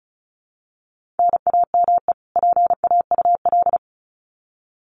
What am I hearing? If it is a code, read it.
Morse code, heard as a keyed tone, about 34 wpm, 700 Hz. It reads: NAME PAUL
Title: Common QSO Phrases 35wpm